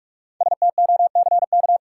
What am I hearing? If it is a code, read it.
Morse code, heard as a keyed tone, about 42 wpm, 700 Hz. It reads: STOCK
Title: Sentences from Top 800 Words - New Word Drill - 45wpm